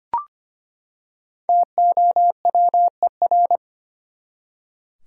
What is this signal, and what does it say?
Morse code, heard as a keyed tone, about 25 wpm, 700 Hz. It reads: TOWER
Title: Five Letter Words - 25wpm